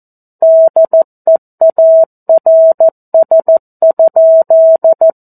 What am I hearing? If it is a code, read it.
Morse code, heard as a keyed tone, about 14 wpm, 650 Hz. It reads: DEARS?
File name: 02-Chapter-02 14 WPM